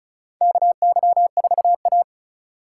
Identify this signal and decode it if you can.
Morse code, heard as a keyed tone, about 36 wpm, 700 Hz. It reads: KY4A
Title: USA Call Signs - 4 Characters - 8x Character Spacing - 35wpm